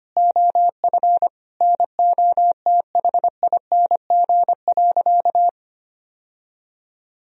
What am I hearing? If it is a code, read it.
Morse code, heard as a keyed tone, about 25 wpm, 700 Hz. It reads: OF NOTHING.
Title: Sentences from Top 1000 Words - Review Part 3 - 25wpm